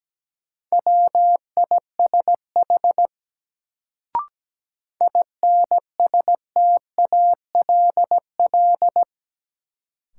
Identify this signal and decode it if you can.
Morse code, heard as a keyed tone, about 17 wpm, 700 Hz. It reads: WISH INSTALL
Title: Sentences from Top 1600 Words - New Word Drill Alt - 17wpm